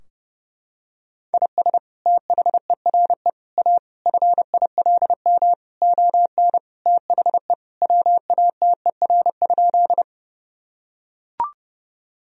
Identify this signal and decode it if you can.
Morse code, heard as a keyed tone, about 30 wpm, 700 Hz. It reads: IS THERE A FILM ON THE WATER?
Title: Sentences from Top 1000 Words - Review Part 3 - 30wpm